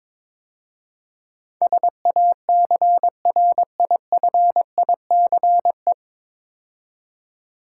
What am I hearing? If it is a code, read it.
Morse code, heard as a keyed tone, about 22 wpm, 700 Hz. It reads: SACRIFICE
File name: 9 Letter Words - First 3 Letters Given - 22wpm